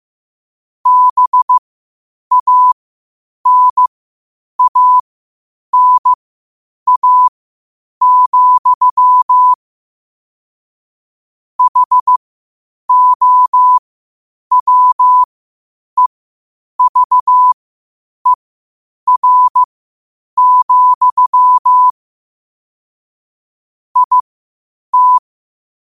Morse code, heard as a keyed tone, about 15 wpm, spaced as for 8 wpm, 1000 Hz. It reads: BANANA, HOWEVER, IT
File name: qotdfiles / 8wpm